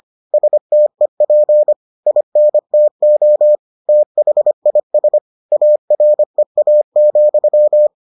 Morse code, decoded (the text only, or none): STEP INTO THIS AREA,